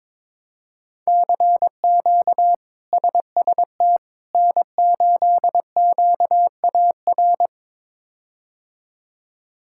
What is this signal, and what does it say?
Morse code, heard as a keyed tone, about 22 wpm, 700 Hz. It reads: CQ SST N8QAR
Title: K1USN SST QSO Practice - 22wpm